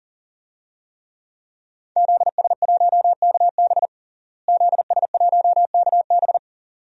Morse code, heard as a keyed tone, about 40 wpm, 700 Hz. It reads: ZS1KB ZS1KB
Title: DXCC-100 Call Signs - 5 Characters - Repeated 2x - 40wpm